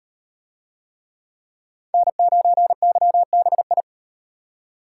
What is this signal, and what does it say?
Morse code, heard as a keyed tone, about 38 wpm, 700 Hz. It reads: N9YBI